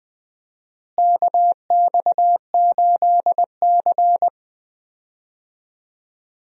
Morse code, heard as a keyed tone, about 20 wpm, 700 Hz. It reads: KX8C
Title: USA Call Signs - 4 Characters - Repeated 2x - 20wpm